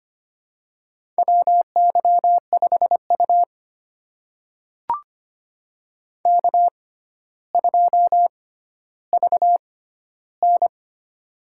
Morse code, heard as a keyed tone, about 25 wpm, spaced as for 13 wpm, 700 Hz. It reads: WY5U K2VN
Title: USA Call Signs - 4 Characters - 6x Character Spacing - 25wpm